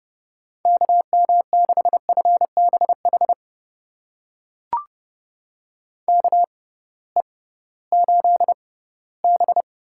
Morse code, heard as a keyed tone, about 30 wpm, spaced as for 18 wpm, 700 Hz. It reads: KM6FBH KE8B
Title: USA Call Signs - 6 Characters - 6x Character Spacing - 30wpm